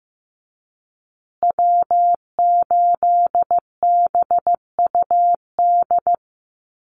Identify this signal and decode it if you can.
Morse code, heard as a keyed tone, about 15 wpm, 700 Hz. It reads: W8BUD
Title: Call Signs Repeated 2x - 15wpm